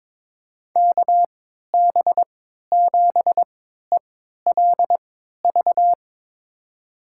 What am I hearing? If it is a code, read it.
Morse code, heard as a keyed tone, about 22 wpm, spaced as for 12 wpm, 700 Hz. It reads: KB7ELV
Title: USA Call Signs - 6 Characters - 3x Character Spacing - 22wpm